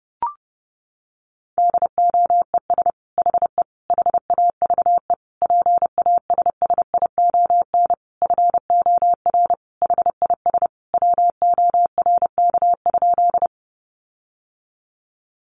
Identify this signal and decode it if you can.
Morse code, heard as a keyed tone, about 30 wpm, 700 Hz. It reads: DOES HE HAVE PASSION FOR HIS WORK?
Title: Sentences from Top 1300 Words 30wpm